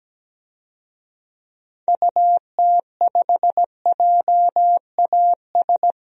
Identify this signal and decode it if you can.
Morse code, heard as a keyed tone, about 17 wpm, 700 Hz. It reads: UT5JAS
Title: DXCC-100 Call Signs - 6 Characters - 17wpm